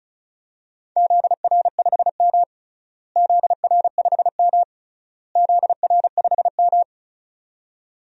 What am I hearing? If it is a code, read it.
Morse code, heard as a keyed tone, about 36 wpm, 700 Hz. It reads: ZR5M ZR5M ZR5M
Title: DXCC-100 Call Signs - 4 Characters - Repeated 3x - 35wpm